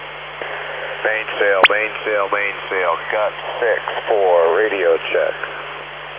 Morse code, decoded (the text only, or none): none